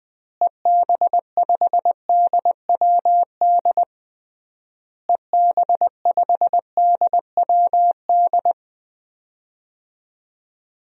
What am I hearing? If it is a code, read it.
Morse code, heard as a keyed tone, about 20 wpm, 700 Hz. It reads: EB5DWD EB5DWD
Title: DXCC-100 Call Signs - 6 Characters - Repeated 3x - 20wpm